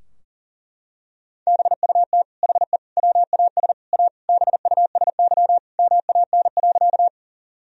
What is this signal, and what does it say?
Morse code, heard as a keyed tone, about 40 wpm, 700 Hz. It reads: BUT HE WAS A BUSY MAN.
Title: Sentences from Top 1000 Words - Review Part 3 - 40wpm